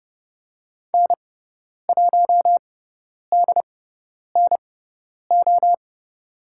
Morse code, heard as a keyed tone, about 30 wpm, spaced as for 10 wpm, 700 Hz. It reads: N1DNO
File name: Lesson 16g - Letter c - Call Signs 30wpm